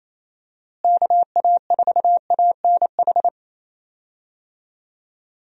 Morse code, heard as a keyed tone, about 28 wpm, 700 Hz. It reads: KA4ANH
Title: USA Call Signs - 6 Characters - Repeated 2x - 28wpm